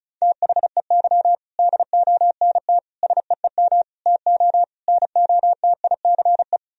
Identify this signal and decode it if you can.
Morse code, heard as a keyed tone, about 36 wpm, 700 Hz. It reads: THEY DONT SEEM TO NOTICE
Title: Sentences from Top 1000 Words - Review Part 2 - 35wpm